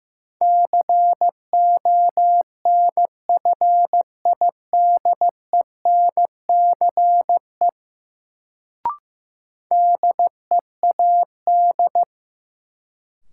Morse code, heard as a keyed tone, about 15 wpm, 700 Hz. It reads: CONFIDENCE DEAD
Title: Sentences from Top 1500 Words - New Word Drill Alt - 15wpm